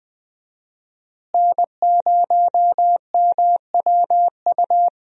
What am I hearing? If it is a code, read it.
Morse code, heard as a keyed tone, about 20 wpm, 700 Hz. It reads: N0MWU
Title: USA Call Signs - 5 Characters - Repeated 3x - 20wpm